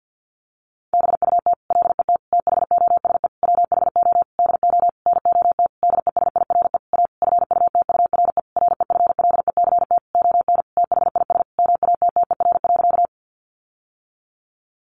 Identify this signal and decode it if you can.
Morse code, heard as a keyed tone, about 50 wpm, 700 Hz. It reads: BUT LET THOSE WHO DO NOT DESIRE A FUTURE REFLECT ON THIS MATTER.
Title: Sentences from Top 1500 Words 50wpm